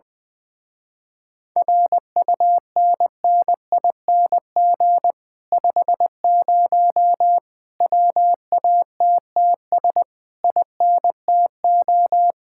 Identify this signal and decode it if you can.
Morse code, heard as a keyed tone, about 20 wpm, 700 Hz. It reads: RUNNING 50 WATTS INTO
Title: Speed-Racing Common QSO Phrases Encore - 17wpm